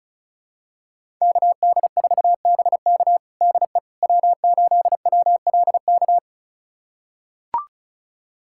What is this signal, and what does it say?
Morse code, heard as a keyed tone, about 34 wpm, 700 Hz. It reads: KD4BK DE W8WLK
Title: USA Callsign DE Callsign 35wpm